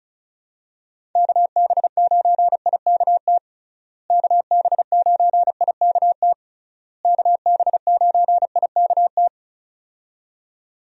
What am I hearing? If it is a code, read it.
Morse code, heard as a keyed tone, about 34 wpm, 700 Hz. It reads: KB9IKT KB9IKT KB9IKT
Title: USA Call Signs - 6 Characters - Repeated 3x - 35wpm